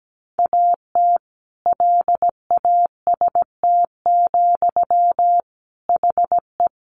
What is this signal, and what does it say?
Morse code, heard as a keyed tone, about 17 wpm, 700 Hz. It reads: AT LAST, HE
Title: Sentences from Top 1400 Words 17wpm